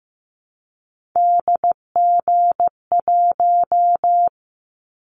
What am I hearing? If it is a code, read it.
Morse code, heard as a keyed tone, about 15 wpm, 700 Hz. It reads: DG1
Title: Callsign Prefixes for Top 25 DXCC - 15wpm